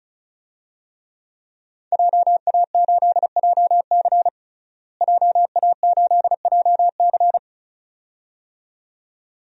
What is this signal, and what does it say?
Morse code, heard as a keyed tone, about 34 wpm, 700 Hz. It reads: JA8JC JA8JC
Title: DXCC-100 Call Signs - 5 Characters - Repeated 2x - 35wpm